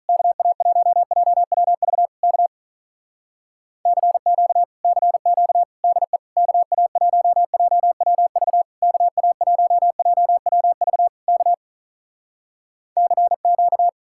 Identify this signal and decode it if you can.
Morse code, heard as a keyed tone, about 46 wpm, 700 Hz. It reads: KA1JWV K CQ CQ DE KA1JWV KA1JWV K CQ